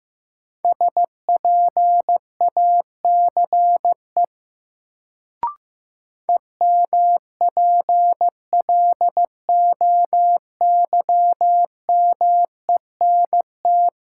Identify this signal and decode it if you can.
Morse code, heard as a keyed tone, about 15 wpm, 700 Hz. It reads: SPACE EMPLOYMENT
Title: Sentences from Top 1500 Words - New Word Drill Alt - 15wpm